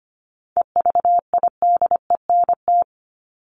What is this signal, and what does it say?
Morse code, heard as a keyed tone, about 25 wpm, 700 Hz. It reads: EVIDENT
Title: LY Suffix - Top 100 Words with Base Words 25wpm